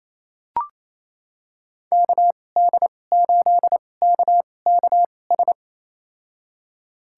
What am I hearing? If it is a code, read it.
Morse code, heard as a keyed tone, about 28 wpm, spaced as for 20 wpm, 700 Hz. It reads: KD8KKS
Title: USA Call Signs - 2x Character Spacing - 28wpm